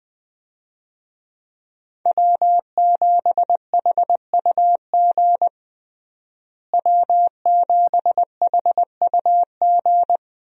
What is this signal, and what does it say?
Morse code, heard as a keyed tone, about 21 wpm, 700 Hz. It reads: W7HUG W7HUG
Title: USA Call Signs - 5 Characters - Repeated 2x - 20wpm